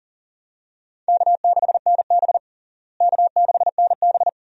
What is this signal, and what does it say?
Morse code, heard as a keyed tone, about 40 wpm, 700 Hz. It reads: K6NB K6NB
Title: USA Call Signs - 4 Characters - Repeated 2x - 40wpm